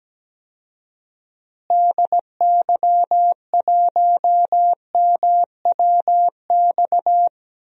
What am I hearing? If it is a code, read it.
Morse code, heard as a keyed tone, about 17 wpm, 700 Hz. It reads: DY1MWX
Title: DXCC-100 Call Signs - 6 Characters - Repeated 3x - 17wpm